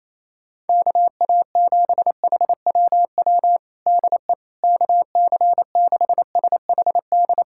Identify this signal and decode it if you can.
Morse code, heard as a keyed tone, about 27 wpm, 700 Hz. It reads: KA7HWW DE KC6SHD
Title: USA Callsign DE Callsign 28wpm